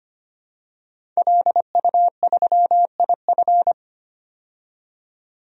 Morse code, heard as a keyed tone, about 25 wpm, 700 Hz. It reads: LU3IF
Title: DXCC-100 Call Signs - 5 Characters - 25wpm